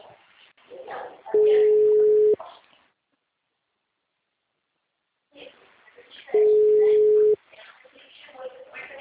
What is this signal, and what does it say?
Ringback tone (Europe, 425 Hz), with speech.
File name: out- -2021- - - .8494